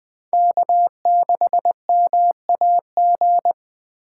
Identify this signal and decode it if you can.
Morse code, heard as a keyed tone, about 21 wpm, 700 Hz. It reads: K6MAG